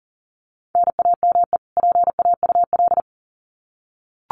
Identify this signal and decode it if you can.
Morse code, heard as a keyed tone, about 40 wpm, 700 Hz. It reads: NAME PAUL
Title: Common QSO Phrases 40wpm